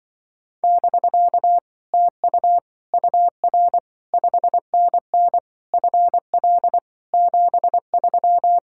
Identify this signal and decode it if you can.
Morse code, heard as a keyed tone, about 24 wpm, 700 Hz. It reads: <BK> TU UR 5NN FL 73
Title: Speed-Racing Common QSO Phrases Encore - 20wpm